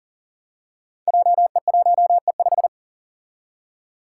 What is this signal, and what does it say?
Morse code, heard as a keyed tone, about 40 wpm, 700 Hz. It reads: JE1E5
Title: Five Challenging Letter-Number Combinations 40wpm